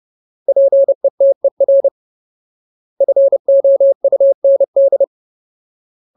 Morse code, heard as a keyed tone, about 30 wpm, 550 Hz. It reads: PETER FOUND
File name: Ultimate Word List / 30WPM Ultimate